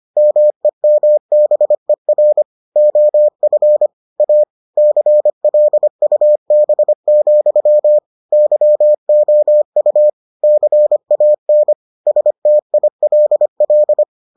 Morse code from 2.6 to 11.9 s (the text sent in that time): OF A CLUB, YOU CAN